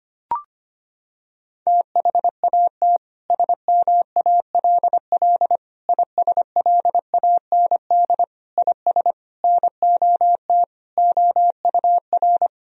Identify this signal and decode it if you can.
Morse code, heard as a keyed tone, about 25 wpm, 700 Hz. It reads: THAT SMALL ISLAND IS NOT OUR
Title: Sentences from Top 1000 Words - Review Part 4 - 25wpm